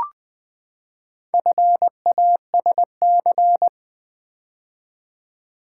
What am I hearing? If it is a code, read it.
Morse code, heard as a keyed tone, about 20 wpm, 700 Hz. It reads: FASC
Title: Build A Word - 4 Letters At A Time - 11-Letter Words - 20wpm